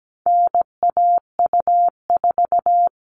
Morse code, heard as a keyed tone, about 17 wpm, 700 Hz. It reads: NAU4
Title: All Cut Numbers - 4-Digit 17wpm